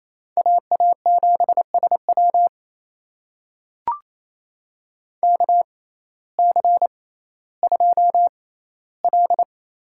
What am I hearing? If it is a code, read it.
Morse code, heard as a keyed tone, about 27 wpm, spaced as for 15 wpm, 700 Hz. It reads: AA7SW KC2L
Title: USA Call Signs - 6x Character Spacing - 28wpm